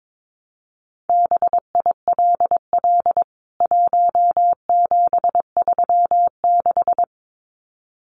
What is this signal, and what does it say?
Morse code, heard as a keyed tone, about 22 wpm, 700 Hz. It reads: BILL 1736
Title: CWT Exchange 22wpm